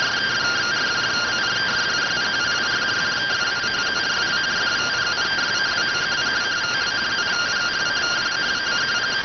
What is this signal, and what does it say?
Two-tone teleprinter signal, about 170 Hz shift.